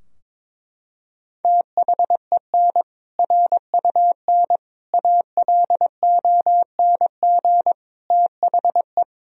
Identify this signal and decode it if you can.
Morse code, heard as a keyed tone, about 22 wpm, 700 Hz. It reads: THEN RUN ALONG THE
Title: Sentences from Top 1000 Words - Review Part 2 - 22wpm